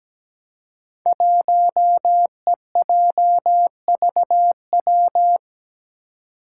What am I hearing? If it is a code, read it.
Morse code, heard as a keyed tone, about 17 wpm, 700 Hz. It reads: 1EJVW
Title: Five Challenging Letter-Number Combinations 17wpm